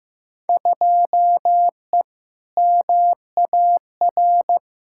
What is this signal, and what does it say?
Morse code, heard as a keyed tone, about 15 wpm, 700 Hz. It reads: 2E MAR
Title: Field Day Exchanges Encore - 15wpm